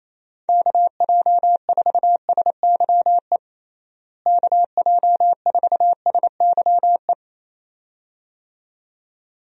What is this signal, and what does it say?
Morse code, heard as a keyed tone, about 28 wpm, 700 Hz. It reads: KJ4SYE KJ4SYE
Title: USA Call Signs - 6 Characters - Repeated 2x - 28wpm